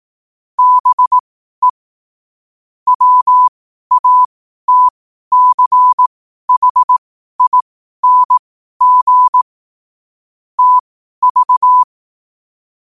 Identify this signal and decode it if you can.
Morse code, heard as a keyed tone, about 18 wpm, spaced as for 12 wpm, 1000 Hz. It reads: BE WATCHING TV
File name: qotdfiles / 12wpm